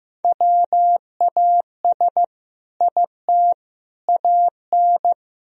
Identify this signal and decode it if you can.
Morse code, heard as a keyed tone, about 15 wpm, 700 Hz. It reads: WAS IT AN
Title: Sentences from Top 1000 Words - Review Part 3 - 15wpm